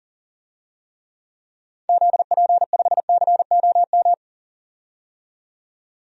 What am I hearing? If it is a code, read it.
Morse code, heard as a keyed tone, about 40 wpm, 700 Hz. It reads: ZP5COM
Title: DXCC-100 Call Signs - 6 Characters - 40wpm